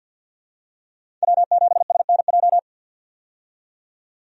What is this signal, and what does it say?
Morse code, heard as a keyed tone, about 48 wpm, 700 Hz. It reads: W7SNJ
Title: USA Call Signs - 5 Characters - 50wpm